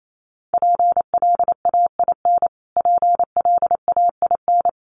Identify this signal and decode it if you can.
Morse code, heard as a keyed tone, about 28 wpm, 700 Hz. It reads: PLAIN PLAIN
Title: Top 500 Words Repeated 28wpm